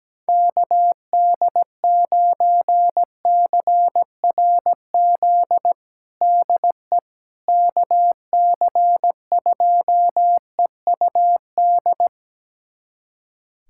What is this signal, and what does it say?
Morse code, heard as a keyed tone, about 17 wpm, 700 Hz. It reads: KD9CRZ DE KC2EUD
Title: USA Callsign DE Callsign 17wpm